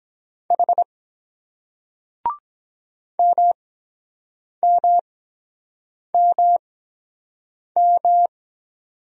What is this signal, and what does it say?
Morse code, heard as a keyed tone, about 23 wpm, spaced as for 7 wpm, 700 Hz. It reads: H MMMM